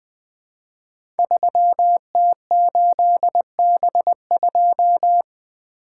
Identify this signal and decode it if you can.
Morse code, heard as a keyed tone, about 20 wpm, 700 Hz. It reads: 3T8B2
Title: Code Groups - 5 Letter:Numbers - 20wpm